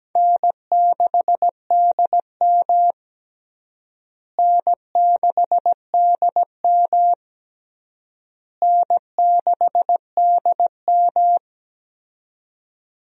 Morse code, heard as a keyed tone, about 17 wpm, 700 Hz. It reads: N6DM N6DM N6DM
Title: USA Call Signs - 4 Characters - Repeated 3x - 17wpm